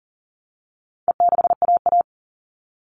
Morse code, heard as a keyed tone, about 40 wpm, 700 Hz. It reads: E6AA
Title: All Cut Numbers - 4-Digit 40wpm